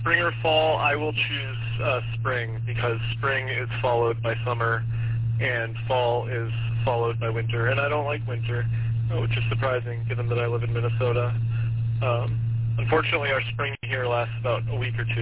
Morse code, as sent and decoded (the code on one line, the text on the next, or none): none